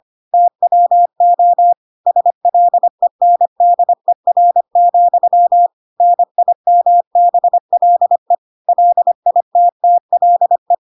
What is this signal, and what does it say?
Morse code, heard as a keyed tone, about 25 wpm, 700 Hz. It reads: TWO SLENDER, NIMBLE LITTLE